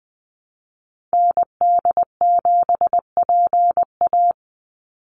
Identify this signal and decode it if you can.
Morse code, heard as a keyed tone, about 20 wpm, 700 Hz. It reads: ND7PA
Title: NA SOTA Activator Callsigns 20wpm